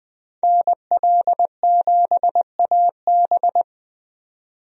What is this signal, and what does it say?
Morse code, heard as a keyed tone, about 20 wpm, 700 Hz. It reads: NL7AB
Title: USA Call Signs - 5 Characters - 20wpm